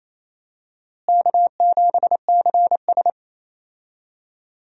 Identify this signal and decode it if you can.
Morse code, heard as a keyed tone, about 28 wpm, 700 Hz. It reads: K7CS
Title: USA Call Signs - 4 Characters - Repeated 2x - 28wpm